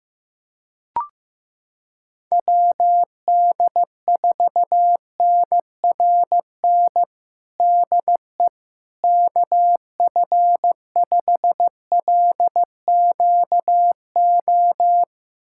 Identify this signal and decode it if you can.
Morse code, heard as a keyed tone, about 15 wpm, 700 Hz. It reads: WD4NRN DE KF5LQO
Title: USA Callsign DE Callsign - 6 Characters - 15wpm